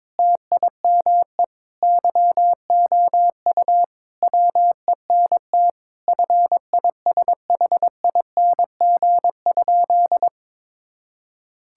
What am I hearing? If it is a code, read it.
Morse code, heard as a keyed tone, about 22 wpm, 700 Hz. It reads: TIME YOU WENT FISHING?
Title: Sentences from Top 1000 Words - Review Part 5 - 22wpm